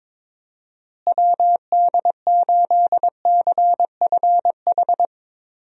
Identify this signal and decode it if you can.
Morse code, heard as a keyed tone, about 22 wpm, 700 Hz. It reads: WD8CFH
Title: USA Call Signs - 6 Characters - Repeated 2x - 22wpm